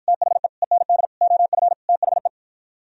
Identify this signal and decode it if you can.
Morse code, heard as a keyed tone, about 54 wpm, 700 Hz. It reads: THE END OF THE